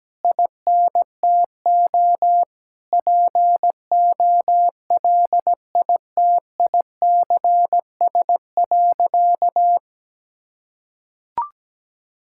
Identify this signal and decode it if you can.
Morse code, heard as a keyed tone, about 17 wpm, 700 Hz. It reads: INTO POLITICS.
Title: Sentences from Top 1000 Words - Review Part 6 - 17wpm